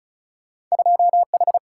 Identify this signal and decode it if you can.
Morse code, heard as a keyed tone, about 36 wpm, 700 Hz. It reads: 2 H
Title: Single-Letter Single-Number 35wpm